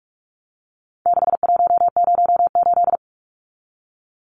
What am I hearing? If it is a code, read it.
Morse code, heard as a keyed tone, about 44 wpm, 700 Hz. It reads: 6108
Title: Numbers - 4-Digit 45wpm